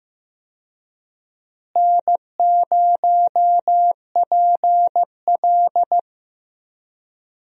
Morse code, heard as a keyed tone, about 16 wpm, 700 Hz. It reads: N0PL
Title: USA Call Signs - 4 Characters - Repeated 2x - 15wpm